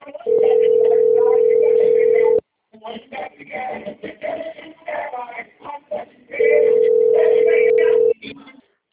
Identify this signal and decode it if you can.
Ringback tone (North America), 440 + 480 Hz.